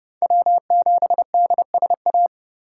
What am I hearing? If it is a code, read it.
Morse code, heard as a keyed tone, about 26 wpm, 700 Hz. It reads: W7DSA